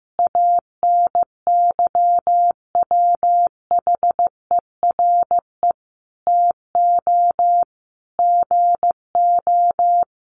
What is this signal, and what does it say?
Morse code, heard as a keyed tone, about 15 wpm, 700 Hz. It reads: ANYWHERE TO GO
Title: Sentences from Top 1400 Words 15wpm